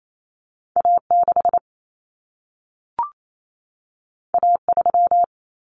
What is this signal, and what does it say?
Morse code, heard as a keyed tone, about 28 wpm, 700 Hz. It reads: A6 A3
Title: Common Cut Numbers - 2-Digit 28wpm